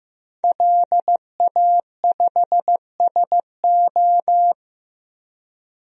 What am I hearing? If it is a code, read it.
Morse code, heard as a keyed tone, about 16 wpm, 700 Hz. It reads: LA5SO
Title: DXCC-100 Call Signs - 5 Characters - Repeated 2x - 15wpm